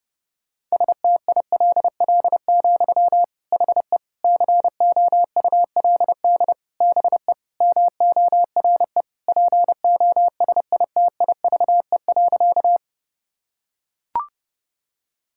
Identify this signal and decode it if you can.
Morse code, heard as a keyed tone, about 32 wpm, 700 Hz. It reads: STILL, HE COULD BE MORE POSITIVE.
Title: Sentences from Top 1000 Words - Review Part 5 - 30wpm